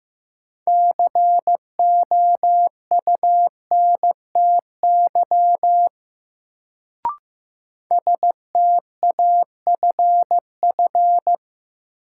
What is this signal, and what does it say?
Morse code, heard as a keyed tone, about 15 wpm, 700 Hz. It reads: COUNTY STAFF
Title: Sentences from Top 1200 Words - New Word Drill - 15wpm